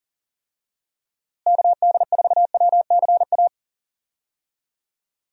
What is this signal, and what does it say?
Morse code, heard as a keyed tone, about 40 wpm, 700 Hz. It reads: KD4WCA
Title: USA Call Signs - 6 Characters - Repeated 2x - 40wpm